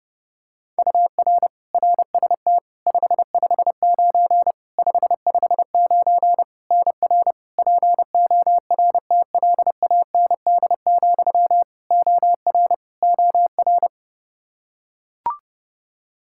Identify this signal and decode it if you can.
Morse code, heard as a keyed tone, about 32 wpm, 700 Hz. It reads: UR RST 559 559 NR PORTLAND, OR OR
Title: Practice QSO - Exchange 1 - Beginning - 30wpm